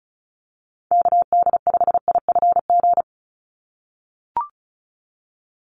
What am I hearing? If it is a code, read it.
Morse code, heard as a keyed tone, about 34 wpm, 700 Hz. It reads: KD5IFG
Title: USA Call Signs 35wpm